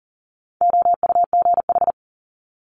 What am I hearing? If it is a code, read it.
Morse code, heard as a keyed tone, about 38 wpm, 700 Hz. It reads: OUGH
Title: Multi-letter Phonemes Quick Fire 40wpm